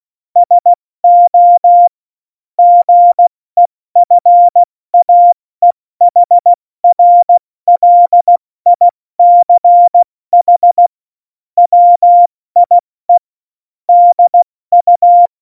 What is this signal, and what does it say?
Morse code, heard as a keyed tone, about 16 wpm, 700 Hz. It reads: SO GEFAEHRLICH WIE DU
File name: Morsekurs / Witze / Witz 19. Dezember 2017, 70 F80